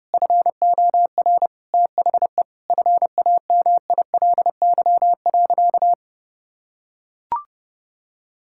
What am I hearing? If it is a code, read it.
Morse code, heard as a keyed tone, about 32 wpm, 700 Hz. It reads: FOR THE FAMILY.
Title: Sentences from Top 1000 Words - Review Part 5 - 30wpm